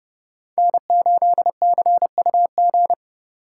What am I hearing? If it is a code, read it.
Morse code, heard as a keyed tone, about 32 wpm, 700 Hz. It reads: N8CUG
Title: USA Call Signs - 5 Characters - 3x Character Spacing - 30wpm